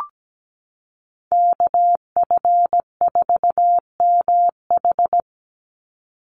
Morse code, heard as a keyed tone, about 17 wpm, 700 Hz. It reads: KF4MH
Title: NA SOTA Chaser Callsigns 17wpm